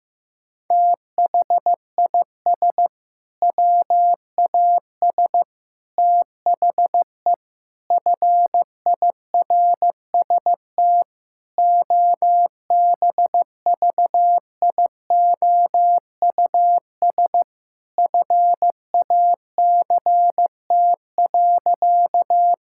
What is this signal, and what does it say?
Morse code, heard as a keyed tone, about 15 wpm, 700 Hz. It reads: THIS WAS THE FIRST OBVIOUS FACT.
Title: Sentences from Top 1600 Words 15wpm